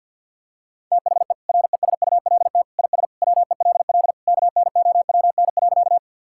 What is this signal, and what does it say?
Morse code, heard as a keyed tone, about 50 wpm, 700 Hz. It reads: THE RESULT IS WELL KNOWN.
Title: Sentences from Top 1000 Words - Review Part 5 - 50wpm